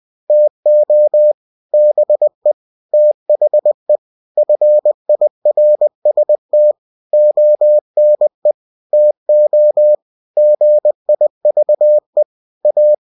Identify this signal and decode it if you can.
Morse code, heard as a keyed tone, about 20 wpm, 600 Hz. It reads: TO BE THE FIRST ONE TO GIVE A